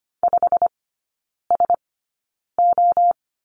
Morse code, heard as a keyed tone, about 25 wpm, spaced as for 9 wpm, 700 Hz. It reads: 5SO
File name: Lesson 17e - Letter u - Three Characters 25wpm